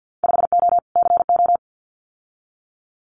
Morse code, heard as a keyed tone, about 50 wpm, 700 Hz. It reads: 5O CO
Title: Winter Field Day Exchanges Encore 50wpm